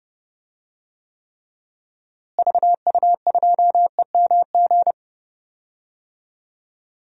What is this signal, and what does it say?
Morse code, heard as a keyed tone, about 30 wpm, 700 Hz. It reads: VU2EMG
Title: DXCC-100 Call Signs - 6 Characters - Repeated 2x - 30wpm